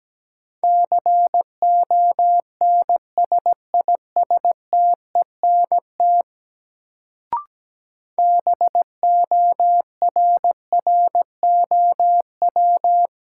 Morse code, heard as a keyed tone, about 17 wpm, 700 Hz. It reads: CONSISTENT BORROW
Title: Sentences from Top 1500 Words - New Word Drill - 17wpm